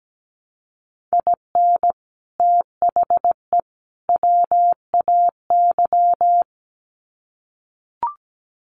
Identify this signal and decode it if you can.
Morse code, heard as a keyed tone, about 17 wpm, 700 Hz. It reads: IN THE WAY
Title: Sets of 3 Words 17wpm